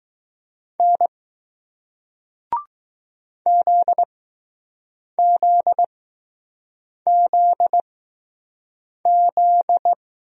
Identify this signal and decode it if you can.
Morse code, heard as a keyed tone, about 20 wpm, spaced as for 6 wpm, 700 Hz. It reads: N ZZZZ